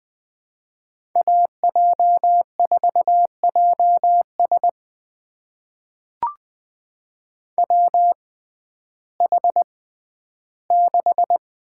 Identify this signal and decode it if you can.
Morse code, heard as a keyed tone, about 20 wpm, spaced as for 12 wpm, 700 Hz. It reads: AJ4JS WH6
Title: USA Call Signs - 5 Characters - 6x Character Spacing - 20wpm